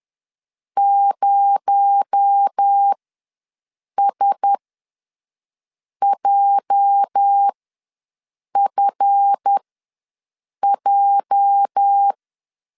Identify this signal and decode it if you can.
Morse code, heard as a keyed tone, about 10 wpm, spaced as for 6 wpm, 800 Hz. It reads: 0SJFJ
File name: LEZIONI / 9 2867 ws 10wpm 4215